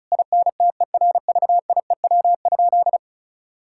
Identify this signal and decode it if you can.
Morse code, heard as a keyed tone, about 36 wpm, 700 Hz. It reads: INTERVIEW?
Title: Sentences from Top 1600 Words 35wpm